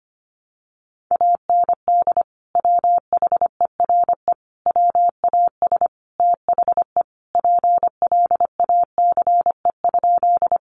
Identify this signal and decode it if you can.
Morse code, heard as a keyed tone, about 25 wpm, 700 Hz. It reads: AND WHERE WAS THE PLACE?
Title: Sentences from Top 200 Words 25wpm